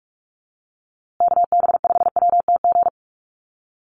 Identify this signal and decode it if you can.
Morse code, heard as a keyed tone, about 44 wpm, 700 Hz. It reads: KB5WTG
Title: USA Call Signs Repeated 3x 45wpm